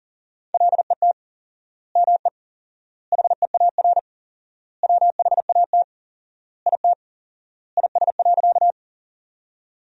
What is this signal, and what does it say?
Morse code, heard as a keyed tone, about 37 wpm, 700 Hz. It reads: LET ME HEAR WHAT IT IS.
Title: 4x Word-Spacing - Sentences from Top 300 Words - 40wpm